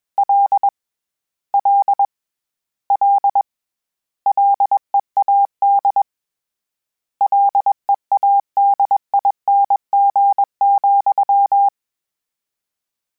Morse code, heard as a keyed tone, about 21 wpm, 800 Hz. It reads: L L L LEAD LEADING,
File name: Combinations / Combinations-mp3s / Combinations@20